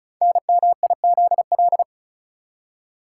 Morse code, heard as a keyed tone, about 36 wpm, 700 Hz. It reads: NMIZL
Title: Code Groups - 5 Letters - 35wpm